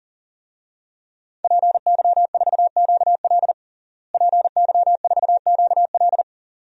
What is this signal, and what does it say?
Morse code, heard as a keyed tone, about 40 wpm, 700 Hz. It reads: PY4QL PY4QL
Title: DXCC-100 Call Signs - 5 Characters - Repeated 2x - 40wpm